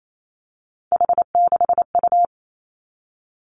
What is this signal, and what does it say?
Morse code, heard as a keyed tone, about 28 wpm, 700 Hz. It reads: H6U
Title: Three Challenging Letter-Number Combinations 28wpm